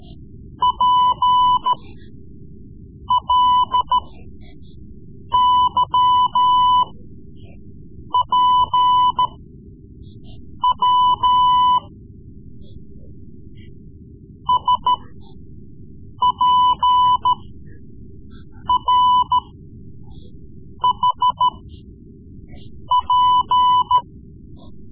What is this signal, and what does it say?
Morse code, heard as a keyed tone, about 14 wpm, spaced as for 5 wpm, 1000 Hz. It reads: PLYPW SPRHP
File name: Morse Lessons / Morse Lessons / Tape-3B